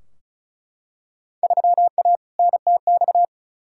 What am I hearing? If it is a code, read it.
Morse code, heard as a keyed tone, about 34 wpm, 700 Hz. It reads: 3A NTX
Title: Field Day Exchanges - 35wpm